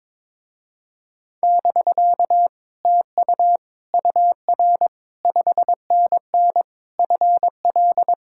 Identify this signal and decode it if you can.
Morse code, heard as a keyed tone, about 21 wpm, 700 Hz. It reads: <BK> TU UR 5NN FL
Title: Speed-Racing Common QSO Phrases - 22wpm